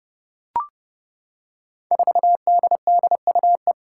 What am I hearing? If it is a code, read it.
Morse code, heard as a keyed tone, about 30 wpm, 700 Hz. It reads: 4DDUE
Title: Five Challenging Letter-Number Combinations 30wpm